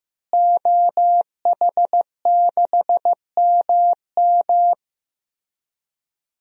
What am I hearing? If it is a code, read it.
Morse code, heard as a keyed tone, about 16 wpm, 700 Hz. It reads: OH6MM
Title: DXCC-100 Call Signs - 5 Characters - Repeated 3x - 15wpm